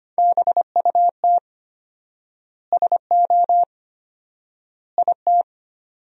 Morse code, heard as a keyed tone, about 25 wpm, 700 Hz. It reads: BUT SO IT
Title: 4x Word-Spacing - Sentences from Top 100 Words - 25wpm